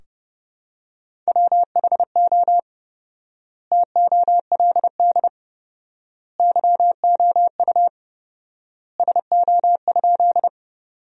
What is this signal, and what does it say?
Morse code, heard as a keyed tone, about 30 wpm, 700 Hz. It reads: WHO TOLD YOU SO?
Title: 4x Word-Spacing - Sentences from Top 400 Words - 30wpm